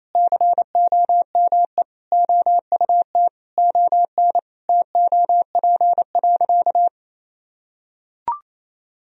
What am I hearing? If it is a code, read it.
Morse code, heard as a keyed tone, about 28 wpm, 700 Hz. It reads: COME OUT ON TOP.
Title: Sentences from Top 1000 Words - Review Part 2 - 28wpm